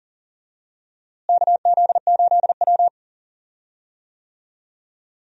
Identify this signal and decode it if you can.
Morse code, heard as a keyed tone, about 40 wpm, 700 Hz. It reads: KZ8W
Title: USA Call Signs - 4 Characters - Repeated 3x - 40wpm